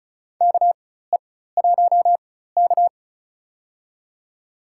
Morse code, heard as a keyed tone, about 36 wpm, spaced as for 16 wpm, 700 Hz. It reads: KE1K
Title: USA Call Signs - 4 Characters - 4x Character Spacing - 35wpm